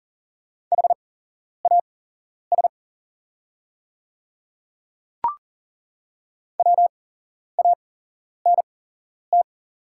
Morse code, heard as a keyed tone, about 40 wpm, spaced as for 11 wpm, 700 Hz. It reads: HAS WANT
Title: 8x Character Spacing - Top 200 Words - 40wpm